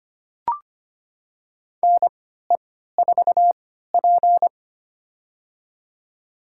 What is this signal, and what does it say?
Morse code, heard as a keyed tone, about 25 wpm, spaced as for 14 wpm, 700 Hz. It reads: NE4P
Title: USA Call Signs - 3x Character Spacing - 25wpm